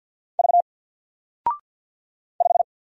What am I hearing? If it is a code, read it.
Morse code, heard as a keyed tone, about 50 wpm, 700 Hz. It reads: V 5
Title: Single Letter-Number 50wpm